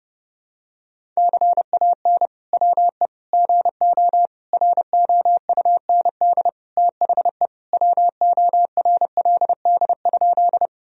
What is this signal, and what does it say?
Morse code, heard as a keyed tone, about 32 wpm, 700 Hz. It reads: CAN WE GO ROUND THE WORLD?
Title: Sentences from Top 1000 Words - Review Part 1 - 30wpm